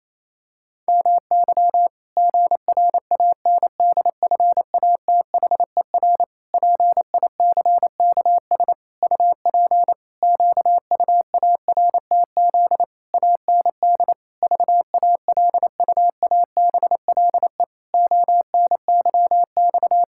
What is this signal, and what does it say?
Morse code, heard as a keyed tone, about 28 wpm, 700 Hz. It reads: MY GRANDFATHER PICKS UP QUARTZ AND VALUABLE ONYX
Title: Pangram Challenge v2 - 28wpm